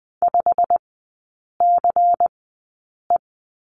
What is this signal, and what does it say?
Morse code, heard as a keyed tone, about 20 wpm, spaced as for 8 wpm, 700 Hz. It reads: 5CE
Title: 191 Lesson 31e - Letter v - Three Characters 20wpm